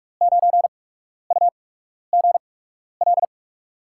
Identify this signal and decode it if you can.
Morse code, heard as a keyed tone, about 42 wpm, spaced as for 12 wpm, 700 Hz. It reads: 9UGL
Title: USA Call Signs - 6 Characters - 8x Character Spacing - 45wpm